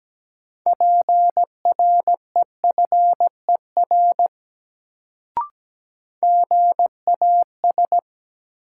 Morse code, heard as a keyed tone, about 17 wpm, 700 Hz. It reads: PREFER GAS
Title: Sentences from Top 1100 Words - New Word Drill Alt - 17wpm